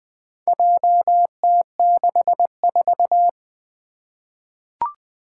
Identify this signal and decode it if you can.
Morse code, heard as a keyed tone, about 20 wpm, 700 Hz. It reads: JT64
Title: Four Challenging Letter-Number Combinations 20wpm